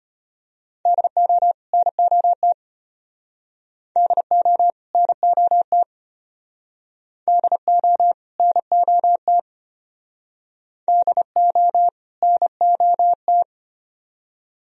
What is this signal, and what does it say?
Morse code, heard as a keyed tone, about 31 wpm, 700 Hz. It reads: DO NOT DO NOT DO NOT DO NOT